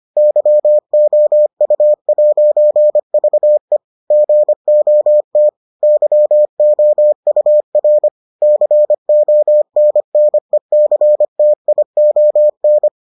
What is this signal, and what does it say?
Morse code, heard as a keyed tone, about 25 wpm, 600 Hz. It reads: YOU'VE GOT YOUR CONNECTION